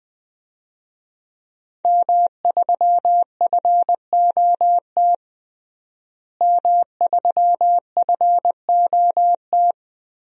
Morse code, heard as a keyed tone, about 20 wpm, 700 Hz. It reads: M3FOT M3FOT
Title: DXCC-100 Call Signs - 5 Characters - Repeated 3x - 20wpm